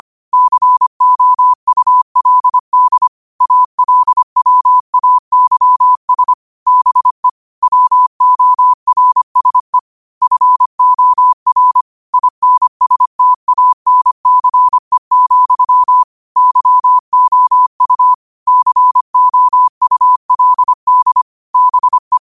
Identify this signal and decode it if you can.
Morse code, heard as a keyed tone, about 25 wpm, 1000 Hz. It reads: COULD ALWAYS BE WORSE FOR INSTANCE, YOU COULD BE